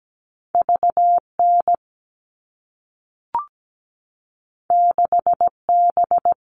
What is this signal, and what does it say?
Morse code, heard as a keyed tone, about 17 wpm, 700 Hz. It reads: VN 6B
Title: All Cut Numbers - 2-Digit 17wpm